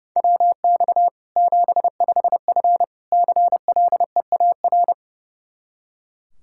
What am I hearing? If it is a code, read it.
Morse code, heard as a keyed tone, about 30 wpm, 700 Hz. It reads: WX 75F CLEAR
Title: Practice QSO - Exchange 2 - Info - 30wpm